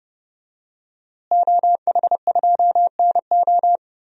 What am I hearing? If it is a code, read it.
Morse code, heard as a keyed tone, about 32 wpm, 700 Hz. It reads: OH2NO
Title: DXCC-100 Call Signs - 5 Characters - Repeated 2x - 30wpm